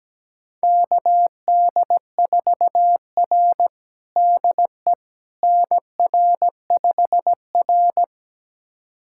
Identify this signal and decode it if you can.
Morse code, heard as a keyed tone, about 17 wpm, 700 Hz. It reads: KD4R DE NR5R
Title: USA Callsign DE Callsign - 4 Characters - 17wpm